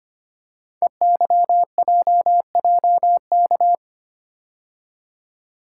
Morse code, heard as a keyed tone, about 26 wpm, 700 Hz. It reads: EYJJK
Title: Code Groups - 5 Letters - 25wpm